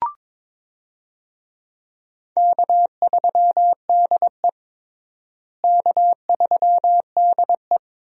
Morse code, heard as a keyed tone, about 21 wpm, 700 Hz. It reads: K3DE K3DE
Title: USA Call Signs - 4 Characters - Repeated 2x - 22wpm